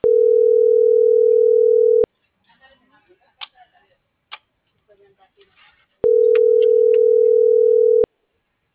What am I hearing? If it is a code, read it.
Ringback tone (North America), 440 + 480 Hz.